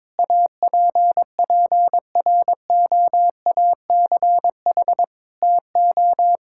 Morse code, heard as a keyed tone, about 22 wpm, 700 Hz. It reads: APPROACH TO
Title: Sentences from Top 1000 Words - Review Part 6 - 22wpm